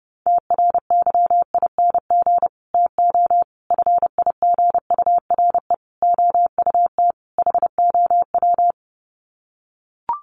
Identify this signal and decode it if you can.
Morse code, heard as a keyed tone, about 30 wpm, 700 Hz. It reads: TRYING TO FIGURE OUT HOW
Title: Sets of 5 Words Encore 30wpm